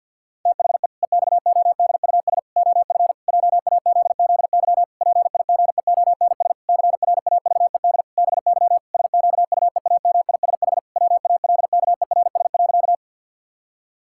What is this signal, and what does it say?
Morse code, heard as a keyed tone, about 50 wpm, 700 Hz. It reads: THE EXODUS OF JAZZY PIGEONS CRAVED BY SQUEAMISH WALKERS.
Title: Pangram Challenge v2 - 50wpm